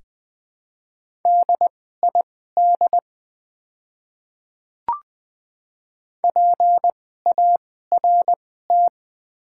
Morse code, heard as a keyed tone, about 20 wpm, spaced as for 14 wpm, 700 Hz. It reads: DID PART
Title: 2x Character Spacing - Top 200 Words - 20wpm